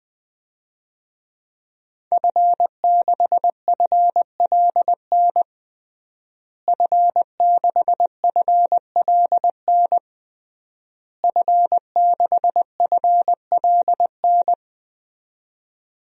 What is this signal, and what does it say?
Morse code, heard as a keyed tone, about 20 wpm, 700 Hz. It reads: F6FLN F6FLN F6FLN
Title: DXCC-100 Call Signs - 5 Characters - Repeated 3x - 20wpm